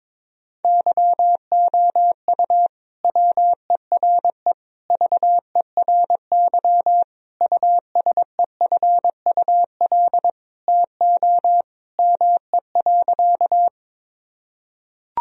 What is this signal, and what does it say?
Morse code, heard as a keyed tone, about 21 wpm, 700 Hz. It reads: YOU WERE VERY USEFUL TO ME.
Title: Sentences from Top 1000 Words - Review Part 4 - 22wpm